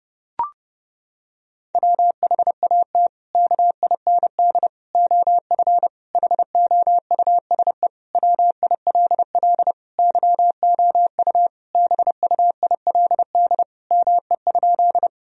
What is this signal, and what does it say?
Morse code, heard as a keyed tone, about 32 wpm, 700 Hz. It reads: WHAT KIND OF HOUSE WILL YOU BUILD ME?
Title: Sentences from Top 1000 Words - Review Part 1 - 30wpm